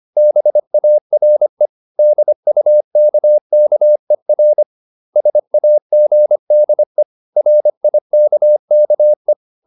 Morse code, heard as a keyed tone, about 25 wpm, 600 Hz. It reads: BARE DUKKER SAGDE RIKKE
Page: Ferie 25 wpm